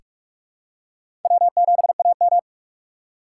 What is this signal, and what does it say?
Morse code, heard as a keyed tone, about 46 wpm, 700 Hz. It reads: W7AM
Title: USA Call Signs - 4 Characters - 4x Character Spacing - 45wpm